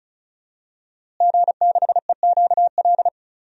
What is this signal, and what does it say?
Morse code, heard as a keyed tone, about 36 wpm, 700 Hz. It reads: G6EQL
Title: DXCC-100 Call Signs - 5 Characters - Repeated 3x - 35wpm